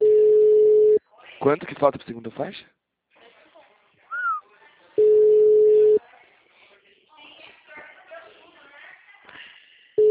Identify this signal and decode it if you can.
Ringback tone (Europe, 425 Hz), with speech.